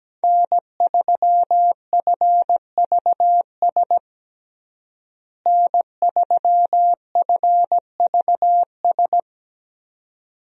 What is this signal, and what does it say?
Morse code, heard as a keyed tone, about 17 wpm, 700 Hz. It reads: N3FVS N3FVS
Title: USA Call Signs - 5 Characters - Repeated 2x - 17wpm